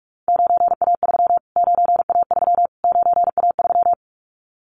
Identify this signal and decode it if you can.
Morse code, heard as a keyed tone, about 44 wpm, 700 Hz. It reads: 9A3 9A3 9A3
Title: Callsign Prefixes for Top 25 DXCC - Repeated 3x - 45wpm